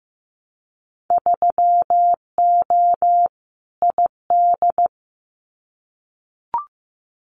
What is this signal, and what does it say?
Morse code, heard as a keyed tone, about 15 wpm, 700 Hz. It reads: 3O ID
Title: Winter Field Day Exchanges Encore 15wpm